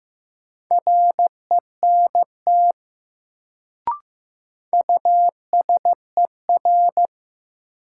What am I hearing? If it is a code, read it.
Morse code, heard as a keyed tone, about 15 wpm, 700 Hz. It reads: RENT USER
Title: Sentences from Top 1000 Words - Review Part 5 - New Word Drill - 15wpm